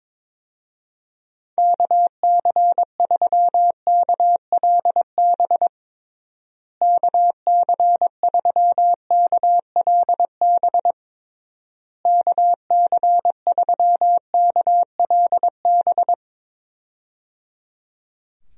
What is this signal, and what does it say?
Morse code, heard as a keyed tone, about 22 wpm, 700 Hz. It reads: KC3KLB KC3KLB KC3KLB
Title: USA Call Signs - 6 Characters - Repeated 3x - 22wpm